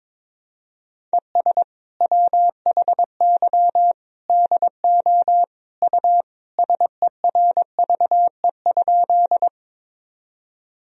Morse code, heard as a keyed tone, about 22 wpm, 700 Hz. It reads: ES WHY DO U SERVE?
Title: Sentences from Top 400 Words and Abbreviations 22wpm